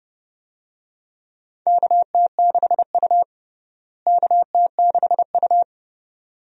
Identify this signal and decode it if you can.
Morse code, heard as a keyed tone, about 30 wpm, 700 Hz. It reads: KT6U KT6U
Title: USA Call Signs - 4 Characters - Repeated 2x - 30wpm